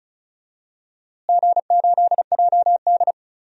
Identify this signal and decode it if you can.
Morse code, heard as a keyed tone, about 36 wpm, 700 Hz. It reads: G8JD
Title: DXCC-100 Call Signs - 4 Characters - Repeated 2x - 35wpm